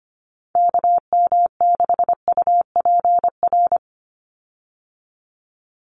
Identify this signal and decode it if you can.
Morse code, heard as a keyed tone, about 25 wpm, 700 Hz. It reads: KM6UPR
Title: USA Call Signs 25wpm